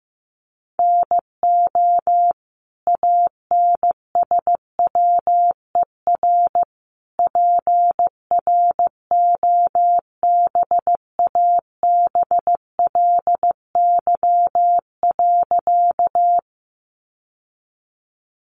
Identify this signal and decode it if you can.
Morse code, heard as a keyed tone, about 15 wpm, 700 Hz. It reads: NO ANSWER PROBABLY.
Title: Sentences from Top 700 Words 15wpm